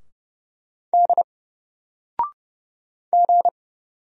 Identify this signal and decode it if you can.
Morse code, heard as a keyed tone, about 30 wpm, 700 Hz. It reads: D G
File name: Single-Letter-Number-30wpm